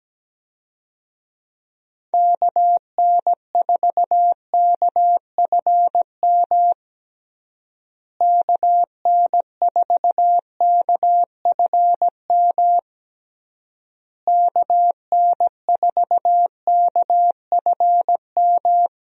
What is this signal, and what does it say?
Morse code, heard as a keyed tone, about 17 wpm, 700 Hz. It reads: KN4KFM KN4KFM KN4KFM
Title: USA Call Signs - 6 Characters - Repeated 3x - 17wpm